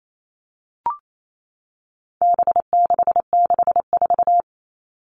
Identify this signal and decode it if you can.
Morse code, heard as a keyed tone, about 28 wpm, 700 Hz. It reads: B664
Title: All Cut Numbers - 4-Digit 28wpm